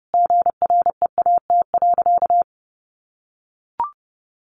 Morse code, heard as a keyed tone, about 30 wpm, 700 Hz. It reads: GREAT.
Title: Sentences from Top 800 Words 30wpm